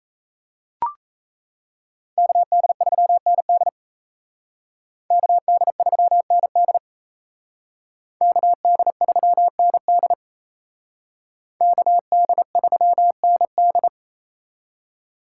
Morse code, heard as a keyed tone, about 35 wpm, 700 Hz. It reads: KD3ND KD3ND KD3ND KD3ND
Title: Speed Racing - USA 2x2 Call Signs 28wpm